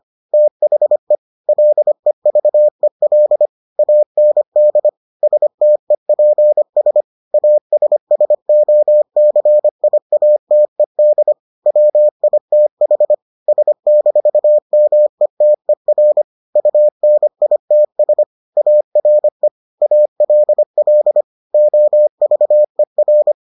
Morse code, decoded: THE LEVEL AND STEPS ASSOCIATED WITH S-METER UNITS ARE ALL OVER

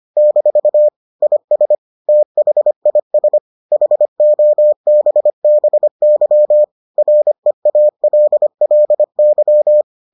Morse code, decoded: - IS THIS HOBBY REALLY

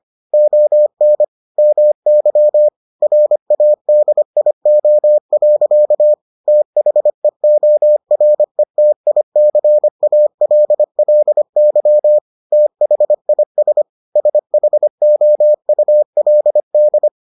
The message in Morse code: ON MY RADIO. THEORETICALLY THIS SHOULD

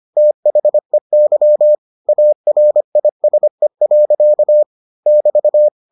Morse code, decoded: THEY ARISE. <BT>

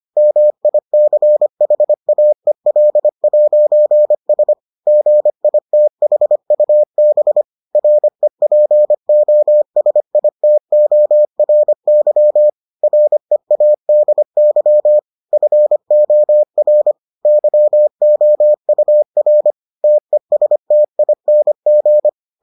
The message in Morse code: MICHAEL'S GITHUB REPOSITORY READY FOR YOUR TESTING

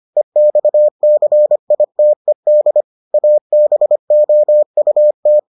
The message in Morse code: EXCITED ABOUT

